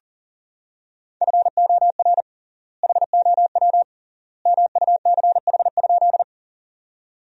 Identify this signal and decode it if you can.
Morse code, heard as a keyed tone, about 40 wpm, 700 Hz. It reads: FOR HOW MUCH?
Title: 3x Word-Spacing - Sentences from Top 200 Words - 40wpm